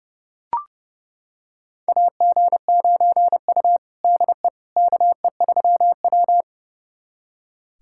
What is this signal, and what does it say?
Morse code, heard as a keyed tone, about 30 wpm, 700 Hz. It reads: AG9U DE KE3W
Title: USA Callsign DE Callsign - 4 Characters - 30wpm